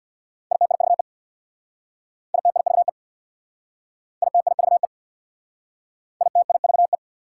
Morse code, heard as a keyed tone, about 65 wpm, 690 Hz. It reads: ITIVE ITIVE ITIVE ITIVE